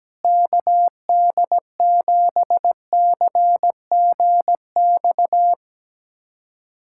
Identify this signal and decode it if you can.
Morse code, heard as a keyed tone, about 17 wpm, 700 Hz. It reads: KD7CGX
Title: USA Callsign DE Callsign - 6 Characters - 17wpm